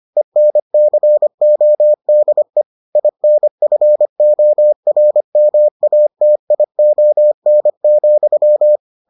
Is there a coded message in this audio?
Yes: Morse code from 0.2 to 8.8 s, heard as a keyed tone, about 25 wpm, 600 Hz. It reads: ENCODE INFORMATION,